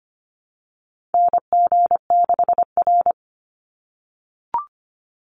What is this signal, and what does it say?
Morse code, heard as a keyed tone, about 25 wpm, 700 Hz. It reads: NG6R
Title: NA SOTA Chaser Callsigns 25wpm